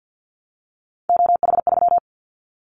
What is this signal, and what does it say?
Morse code, heard as a keyed tone, about 50 wpm, 700 Hz. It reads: OH3
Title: Callsign Prefixes for Top 25 DXCC - Repeated 3x - 50wpm